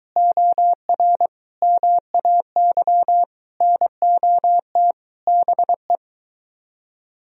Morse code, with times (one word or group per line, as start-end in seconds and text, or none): 0.2-1.3 OR
1.6-3.2 MAY
3.6-4.9 NOT
5.3-6.0 BE